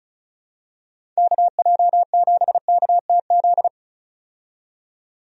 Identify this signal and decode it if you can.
Morse code, heard as a keyed tone, about 34 wpm, 700 Hz. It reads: KJ7KTZ
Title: USA Call Signs - 6 Characters - 35wpm